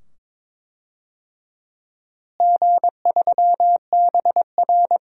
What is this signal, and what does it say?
Morse code, heard as a keyed tone, about 22 wpm, 700 Hz. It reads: G3BR
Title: DXCC-100 Call Signs - 4 Characters - Repeated 2x - 22wpm